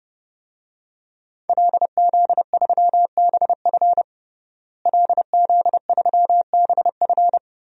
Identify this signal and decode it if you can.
Morse code, heard as a keyed tone, about 32 wpm, 700 Hz. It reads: LZ3BF LZ3BF
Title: DXCC-100 Call Signs - 5 Characters - Repeated 3x - 30wpm